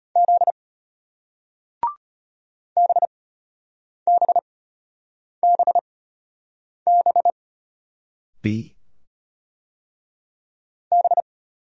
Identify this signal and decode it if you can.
Morse code, heard as a keyed tone, about 33 wpm, spaced as for 8 wpm, 700 Hz. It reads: Z BBBB B